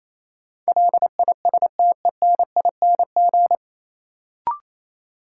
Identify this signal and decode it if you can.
Morse code, heard as a keyed tone, about 28 wpm, 700 Hz. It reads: LISTENING
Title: Sets of 5 Words - New Word Drill - 28wpm